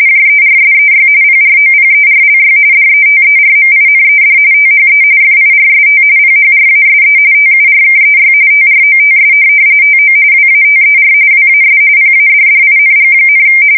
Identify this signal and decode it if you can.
RTTY, about 45.45 baud. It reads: JERSEY PROPERLY DETERMINED THE FIRST PHRASE TURKEY TROTS TO WATER WAS PADDING,
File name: fsk170 / USS-New-Jersey